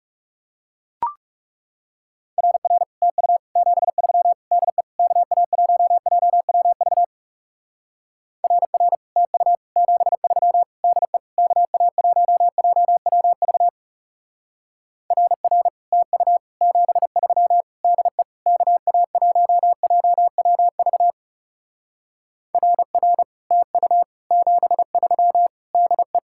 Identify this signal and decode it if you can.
Morse code, heard as a keyed tone, about 37 wpm, 700 Hz. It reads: RR TU 73 DE KA1JWV RR TU 73 DE KA1JWV RR TU 73 DE KA1JWV RR TU 73 DE
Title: Speed-Racing Common QSO Phrases - 30wpm